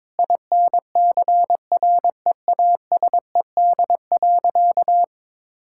Morse code, heard as a keyed tone, about 22 wpm, 700 Hz. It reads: INCREASED.
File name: practice / 22wpm / Sentences with 8-Word Phrases - 22wpm